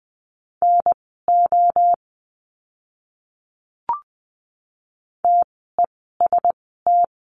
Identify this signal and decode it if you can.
Morse code, heard as a keyed tone, about 20 wpm, spaced as for 14 wpm, 700 Hz. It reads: NO TEST
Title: 068 Lesson 12f - Letter d - Words 20wpm